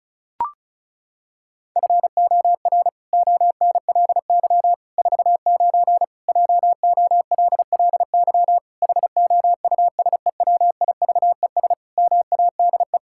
Morse code, heard as a keyed tone, about 36 wpm, 700 Hz. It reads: FOR ONLY 49 JOLLY HOUSEWIVES MADE
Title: Pangram Challenge v2 - 35wpm